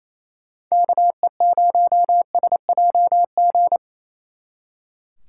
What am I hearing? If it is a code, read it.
Morse code, heard as a keyed tone, about 34 wpm, 700 Hz. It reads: KE0SJG
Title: Speed Racing - USA Call Signs 28wpm